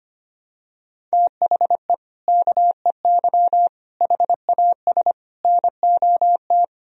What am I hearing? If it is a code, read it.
Morse code, heard as a keyed tone, about 25 wpm, 700 Hz. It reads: THE KEY HAS NOT
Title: Sentences from Top 1000 Words - Review Part 4 - 25wpm